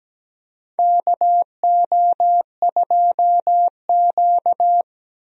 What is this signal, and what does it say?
Morse code, heard as a keyed tone, about 17 wpm, 700 Hz. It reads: KO2Q
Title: USA Call Signs - 4 Characters - Repeated 2x - 17wpm